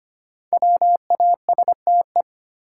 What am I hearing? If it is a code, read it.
Morse code, heard as a keyed tone, about 25 wpm, 700 Hz. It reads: WASTE
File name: 5 Letter Words - First 3 Letters Given - 25wpm